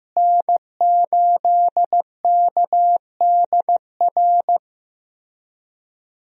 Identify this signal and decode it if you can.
Morse code, heard as a keyed tone, about 15 wpm, 700 Hz. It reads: N8KDR
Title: USA Call Signs - 5 Characters - Repeated 3x - 15wpm